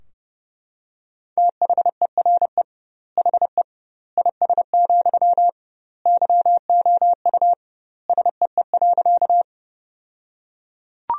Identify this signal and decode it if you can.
Morse code, heard as a keyed tone, about 30 wpm, 700 Hz. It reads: THERE HE IS, YOU SEE.
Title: 2x Word-Spacing - Sentences from Top 100 Words - 30wpm